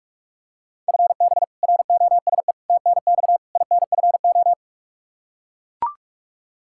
Morse code, heard as a keyed tone, about 44 wpm, 700 Hz. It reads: FB ROSE TNX INFO
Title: Practice QSO - Exchange 3 - More Info - 45wpm